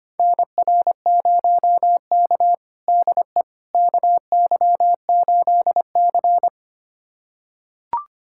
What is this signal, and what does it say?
Morse code, heard as a keyed tone, about 25 wpm, 700 Hz. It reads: NR0K DE KY8C
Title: USA Callsign DE Callsign - 4 Characters - 25wpm